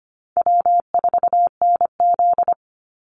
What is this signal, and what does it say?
Morse code, heard as a keyed tone, about 25 wpm, 700 Hz. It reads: W4NZ
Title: Easy Call Signs - 25wpm